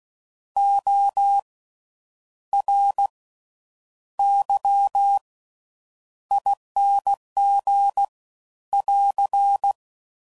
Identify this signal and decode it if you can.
Morse code, heard as a keyed tone, about 16 wpm, 800 Hz. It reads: O R Y ING <AR>
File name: Slow With Faster Ings